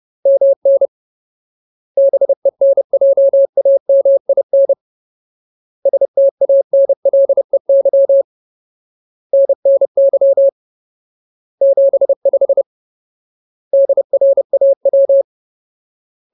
Morse code, heard as a keyed tone, about 30 wpm, 550 Hz. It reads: MN BENJAMIN STANLEY NNY 75 DRAW
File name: Ultimate Word List / 30WPM Ultimate